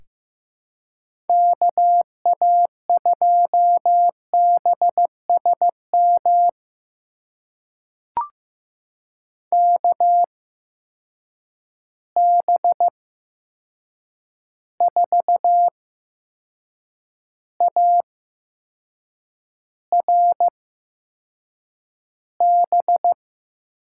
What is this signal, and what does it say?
Morse code, heard as a keyed tone, about 15 wpm, spaced as for 6 wpm, 700 Hz. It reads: KA2BSM KB4ARB
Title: USA Call Signs - 6 Characters - 8x Character Spacing - 15wpm